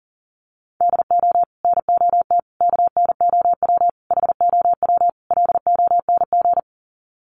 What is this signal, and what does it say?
Morse code, heard as a keyed tone, about 40 wpm, 700 Hz. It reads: DO NOT KNOW HOW LONG
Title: Sets of 5 Words Encore2 40wpm